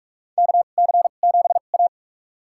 Morse code, heard as a keyed tone, about 46 wpm, spaced as for 32 wpm, 700 Hz. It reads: KC7A
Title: USA Call Signs - 4 Characters - 2x Character Spacing - 45wpm